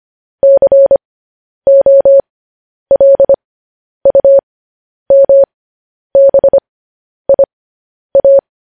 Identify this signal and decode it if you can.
Morse code, heard as a keyed tone, about 25 wpm, spaced as for 10 wpm, 550 Hz. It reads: COLUMBIA